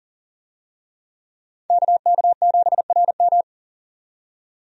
Morse code, heard as a keyed tone, about 40 wpm, 700 Hz. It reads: KK7RM
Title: USA Call Signs - 5 Characters - 40wpm